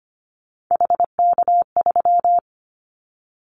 Morse code, heard as a keyed tone, about 25 wpm, 700 Hz. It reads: HK3
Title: Callsign Prefixes for Top 100 DXCC - 25wpm